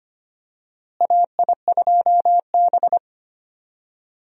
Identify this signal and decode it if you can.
Morse code, heard as a keyed tone, about 25 wpm, 700 Hz. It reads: AI2B
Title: USA Call Signs - 4 Characters - Repeated 2x - 25wpm